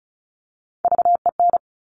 Morse code, heard as a keyed tone, about 36 wpm, 700 Hz. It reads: VEN
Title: Top Three Letter Combinations in Words 35wpm